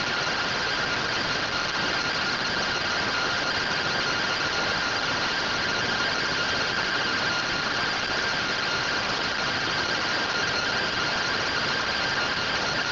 Two-tone teleprinter signal, about 170 Hz shift.